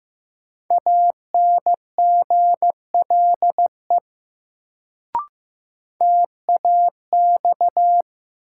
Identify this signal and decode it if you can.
Morse code, heard as a keyed tone, about 15 wpm, 700 Hz. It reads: ANGLE TAX
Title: Sentences from Top 1500 Words - New Word Drill Alt - 15wpm